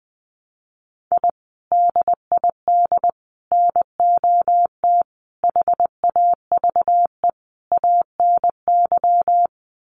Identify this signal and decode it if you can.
Morse code, heard as a keyed tone, about 20 wpm, 700 Hz. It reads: I DID NOT HAVE ANY
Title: Sets of 5 Words Encore 20wpm